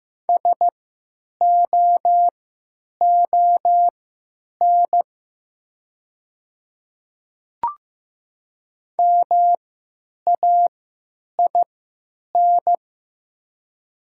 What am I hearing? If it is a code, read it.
Morse code, heard as a keyed tone, about 15 wpm, spaced as for 9 wpm, 700 Hz. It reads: SOON MAIN
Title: 3x Character Spacing - Top 500 Words - 15wpm